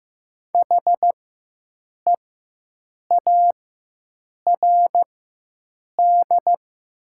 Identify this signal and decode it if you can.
Morse code, heard as a keyed tone, about 15 wpm, spaced as for 7 wpm, 700 Hz. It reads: HEARD
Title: 4x Character Spacing - Top 500 Words - 15wpm